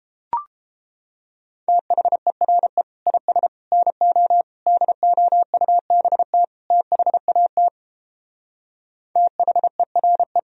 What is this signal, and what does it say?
Morse code, heard as a keyed tone, about 29 wpm, 700 Hz. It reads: THERE IS NO DOUBT THAT THERE